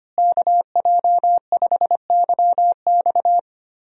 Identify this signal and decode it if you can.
Morse code, heard as a keyed tone, about 25 wpm, 700 Hz. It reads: KJ5YX
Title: USA Call Signs - 5 Characters - Repeated 2x - 25wpm